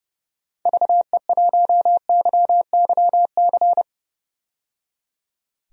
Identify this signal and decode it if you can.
Morse code, heard as a keyed tone, about 30 wpm, 700 Hz. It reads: VE1YYC
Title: DXCC-100 Call Signs - 6 Characters - 30wpm